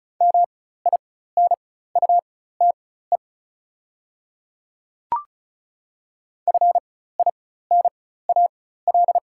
Morse code, heard as a keyed tone, about 34 wpm, spaced as for 16 wpm, 690 Hz. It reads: MINUTE FINAL
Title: 4x Character Spacing - Top 500 Words - 35wpm